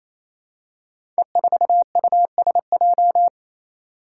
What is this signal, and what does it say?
Morse code, heard as a keyed tone, about 27 wpm, 700 Hz. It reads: E4USJ
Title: Five Challenging Letter-Number Combinations 28wpm